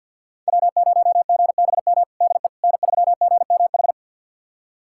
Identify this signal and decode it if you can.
Morse code, heard as a keyed tone, about 51 wpm, 700 Hz. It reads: W0GBM DE N3GMH
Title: USA Callsign DE Callsign - 5 Characters - 50wpm